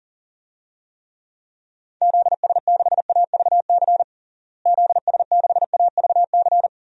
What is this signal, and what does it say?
Morse code, heard as a keyed tone, about 40 wpm, 700 Hz. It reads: ZS6AVC ZS6AVC
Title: DXCC-100 Call Signs - 6 Characters - Repeated 3x - 40wpm